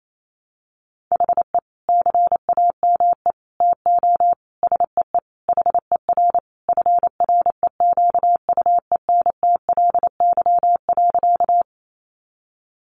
Morse code, heard as a keyed tone, about 28 wpm, 700 Hz. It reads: HE CAME TO SEE HER FREQUENTLY.
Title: Sentences from Top 1300 Words 28wpm